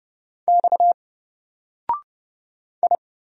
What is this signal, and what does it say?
Morse code, heard as a keyed tone, about 29 wpm, 700 Hz. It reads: X I
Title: Single Character (Letter:Number:Punctuation:Prosign) 30wpm